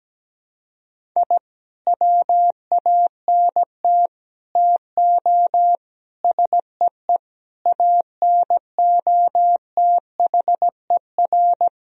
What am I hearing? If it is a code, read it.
Morse code, heard as a keyed tone, about 17 wpm, 700 Hz. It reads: I WANT TO SEE ANOTHER
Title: Sentences from Top 1000 Words - Review Part 1 - 17wpm